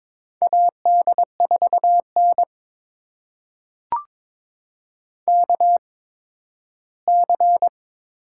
Morse code, heard as a keyed tone, about 22 wpm, spaced as for 8 wpm, 700 Hz. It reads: AD4NKC
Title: USA Call Signs - 8x Character Spacing - 22wpm